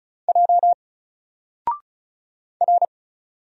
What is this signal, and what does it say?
Morse code, heard as a keyed tone, about 33 wpm, 700 Hz. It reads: J R
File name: Single-Letter-Number-35wpm